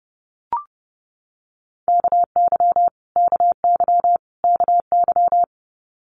Morse code, heard as a keyed tone, about 30 wpm, 700 Hz. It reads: KY KY KY
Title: Multi-letter Phonemes Repeated 3x 30wpm